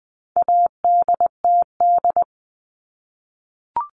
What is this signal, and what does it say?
Morse code, heard as a keyed tone, about 20 wpm, 700 Hz. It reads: ADTD
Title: All Cut Numbers - 4-Digit 20wpm